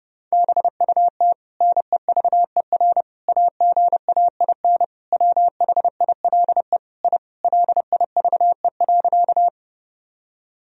Morse code, heard as a keyed tone, about 30 wpm, 700 Hz. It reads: BUT NEVER AGAIN WHILE I LIVE.
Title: Sentences from Top 1000 Words - Review Part 1 - 30wpm